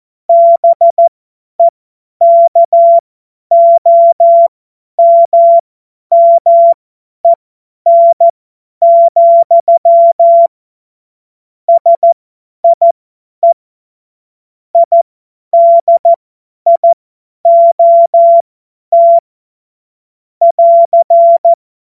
Morse code, heard as a keyed tone, about 14 wpm, spaced as for 10 wpm, 680 Hz. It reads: BEKOMMEN, SIE IDIOT <AR>
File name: Morsekurs / Witze / Witz 24 Juni 2025 50 F70